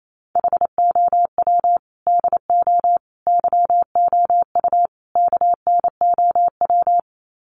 Morse code, heard as a keyed tone, about 28 wpm, 700 Hz. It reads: HOW DO YOU KNOW
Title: Sets of 4 Words 28wpm